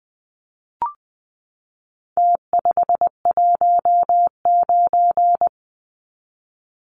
Morse code, heard as a keyed tone, about 20 wpm, 700 Hz. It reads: T519
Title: Numbers - Contest Style - 4-Digit 20wpm